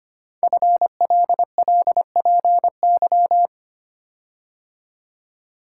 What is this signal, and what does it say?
Morse code, heard as a keyed tone, about 25 wpm, 700 Hz. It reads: FLLPY
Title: Code Groups - 5 Letters - 25wpm